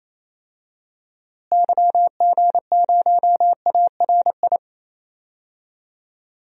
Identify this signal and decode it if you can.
Morse code, heard as a keyed tone, about 28 wpm, 700 Hz. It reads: YG0ARI
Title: DXCC-100 Call Signs - 6 Characters - Repeated 3x - 28wpm